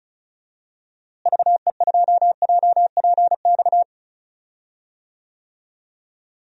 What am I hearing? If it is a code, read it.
Morse code, heard as a keyed tone, about 36 wpm, 700 Hz. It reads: VE2JPX
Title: DXCC-100 Call Signs - 6 Characters - Repeated 2x - 35wpm